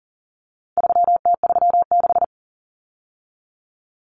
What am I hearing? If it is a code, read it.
Morse code, heard as a keyed tone, about 40 wpm, 700 Hz. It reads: 3T36
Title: Common Cut Numbers - 4-Digit 40wpm